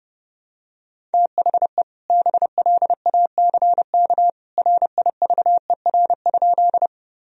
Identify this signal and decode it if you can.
Morse code, heard as a keyed tone, about 32 wpm, 700 Hz. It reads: THE BLACK RIVER?
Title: Sentences from Top 1000 Words - Review Part 2 - 30wpm